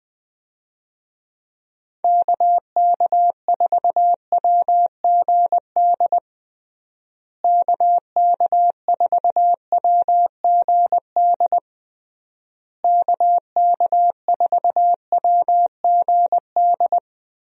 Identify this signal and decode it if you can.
Morse code, heard as a keyed tone, about 21 wpm, 700 Hz. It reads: KK4WGD KK4WGD KK4WGD
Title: USA Call Signs - 6 Characters - Repeated 3x - 20wpm